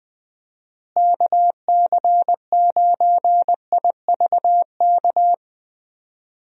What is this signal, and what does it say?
Morse code, heard as a keyed tone, about 20 wpm, 700 Hz. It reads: KC9IVK
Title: USA Call Signs - 6 Characters - Repeated 3x - 20wpm